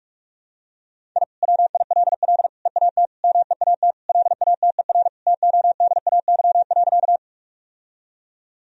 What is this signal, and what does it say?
Morse code, heard as a keyed tone, about 45 wpm, 700 Hz. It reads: I WILL EAT MEAT LATER TODAY.
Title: Sentences from Top 1000 Words - Review Part 3 - 45wpm